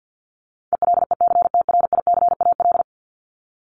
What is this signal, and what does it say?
Morse code, heard as a keyed tone, about 48 wpm, 700 Hz. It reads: ELECTRICAL
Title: Top 751-1000 Words 50wpm